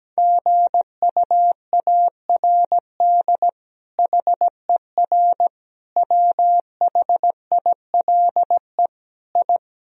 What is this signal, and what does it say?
Morse code, heard as a keyed tone, about 17 wpm, 700 Hz. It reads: GUARD HER WHILE I
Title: Sentences from Top 1000 Words - Review Part 4 - 17wpm